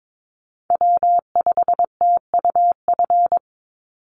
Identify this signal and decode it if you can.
Morse code, heard as a keyed tone, about 22 wpm, 700 Hz. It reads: W5TUF
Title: Call Signs Repeated 3x - 22wpm